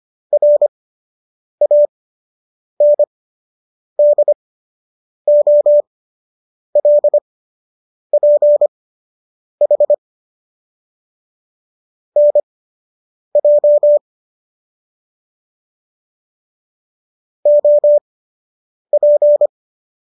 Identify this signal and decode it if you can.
Morse code, heard as a keyed tone, about 25 wpm, spaced as for 8 wpm, 590 Hz. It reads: RANDOLPH NJ OP